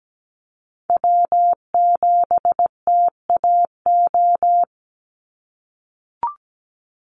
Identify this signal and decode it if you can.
Morse code, heard as a keyed tone, about 17 wpm, 700 Hz. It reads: W7TAO
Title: NA SOTA Chaser Callsigns 17wpm